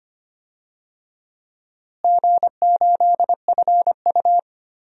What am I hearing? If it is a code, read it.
Morse code, heard as a keyed tone, about 25 wpm, 700 Hz. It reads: G8FU
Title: DXCC-100 Call Signs - 4 Characters - Repeated 2x - 25wpm